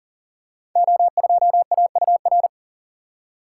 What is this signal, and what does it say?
Morse code, heard as a keyed tone, about 38 wpm, 700 Hz. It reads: O2AUR
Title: Code Groups - 5 Letter:Numbers - 40wpm